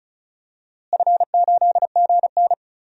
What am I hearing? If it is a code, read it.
Morse code, heard as a keyed tone, about 36 wpm, 700 Hz. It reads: F8GN
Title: DXCC-100 Call Signs - 4 Characters - Repeated 3x - 35wpm